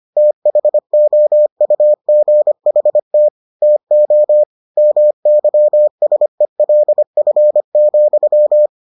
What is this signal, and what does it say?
Morse code, heard as a keyed tone, about 25 wpm, 600 Hz. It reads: THOUGHT TO MYSELF,